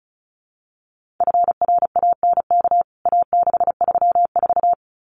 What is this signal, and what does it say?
Morse code, heard as a keyed tone, about 34 wpm, 700 Hz. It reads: FRANK A634
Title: CWT Exchange 35wpm